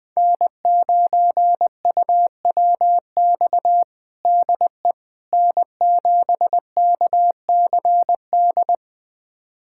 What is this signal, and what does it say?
Morse code, heard as a keyed tone, about 19 wpm, 700 Hz. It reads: N9UWX DE N7KCD
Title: USA Callsign DE Callsign - 5 Characters - 20wpm